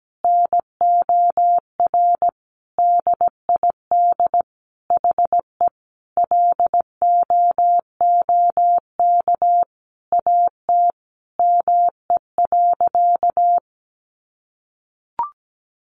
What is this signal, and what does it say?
Morse code, heard as a keyed tone, about 17 wpm, 700 Hz. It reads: NOR DID HE LOOK AT ME.
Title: Sentences from Top 1500 Words 17wpm